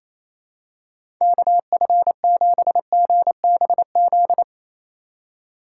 Morse code, heard as a keyed tone, about 27 wpm, 700 Hz. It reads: KF7GBZ
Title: USA Call Signs - 6 Characters - 4x Character Spacing - 28wpm